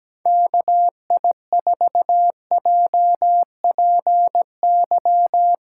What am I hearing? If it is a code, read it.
Morse code, heard as a keyed tone, about 17 wpm, 700 Hz. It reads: KI4JPY
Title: USA Callsign DE Callsign 17wpm